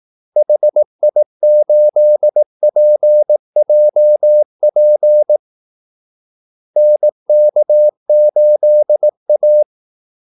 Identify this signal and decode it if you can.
Morse code, heard as a keyed tone, about 18 wpm, 600 Hz. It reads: HI8PJP NK8A
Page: Kaldesignaler 18 wpm